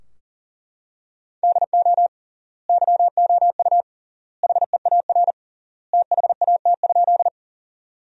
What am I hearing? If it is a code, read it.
Morse code, heard as a keyed tone, about 37 wpm, 700 Hz. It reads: DO YOU HEAR THAT?
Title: 3x Word-Spacing - Sentences from Top 300 Words - 40wpm